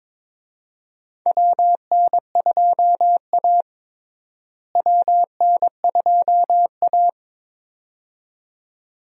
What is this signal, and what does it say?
Morse code, heard as a keyed tone, about 21 wpm, 700 Hz. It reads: WN2A WN2A
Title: USA Call Signs - 4 Characters - Repeated 2x - 22wpm